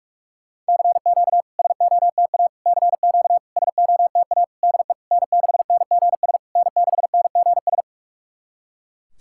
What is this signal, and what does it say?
Morse code, heard as a keyed tone, about 43 wpm, 700 Hz. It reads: CQ SOTA CQ SOTA DE N6NGS N6NGS
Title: SOTA QSO Practice 45wpm